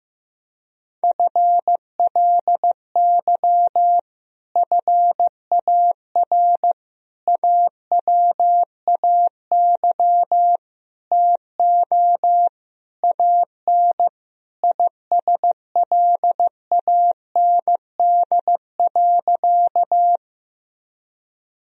Morse code, heard as a keyed tone, about 16 wpm, 700 Hz. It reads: FLY FAR AWAY TO AN ISLAND.
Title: Sentences from Top 1000 Words - Review Part 2 - 15wpm